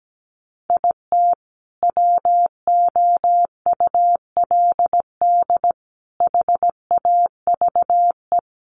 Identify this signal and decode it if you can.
Morse code, heard as a keyed tone, about 17 wpm, 700 Hz. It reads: IT WOULD HAVE
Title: Sentences from Top 1100 Words 17wpm